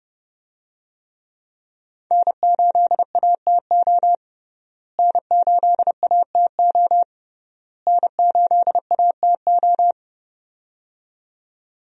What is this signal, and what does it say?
Morse code, heard as a keyed tone, about 30 wpm, 700 Hz. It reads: N8ATO N8ATO N8ATO
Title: USA Call Signs - 5 Characters - Repeated 3x - 30wpm